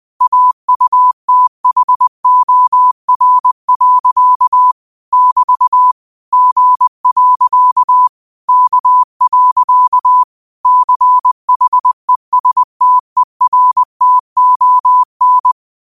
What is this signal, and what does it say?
Morse code, heard as a keyed tone, about 20 wpm, 1000 Hz. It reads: AUTHOR. <BT> G. K. CHESTERTON